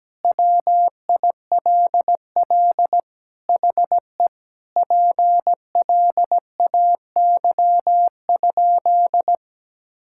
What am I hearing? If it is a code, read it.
Morse code, heard as a keyed tone, about 17 wpm, 700 Hz. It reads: WILL HE PLAY?
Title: Sentences from Top 1000 Words - Review Part 6 - 17wpm